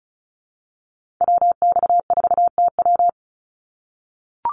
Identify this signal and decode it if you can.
Morse code, heard as a keyed tone, about 34 wpm, 700 Hz. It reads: WX4TW
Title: NA SOTA Activator Callsigns 35wpm